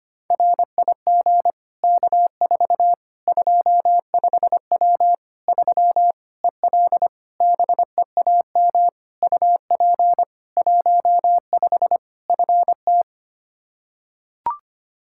Morse code, heard as a keyed tone, about 25 wpm, 700 Hz. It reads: RIG K4 25W 3 EL BEAM UP 15 FT
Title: Practice QSO - Exchange 2 - Info - 25wpm